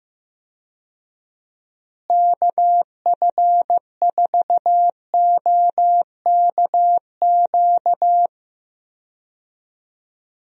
Morse code, heard as a keyed tone, about 16 wpm, 700 Hz. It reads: KF4OKQ
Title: USA Call Signs - 6 Characters - Repeated 3x - 15wpm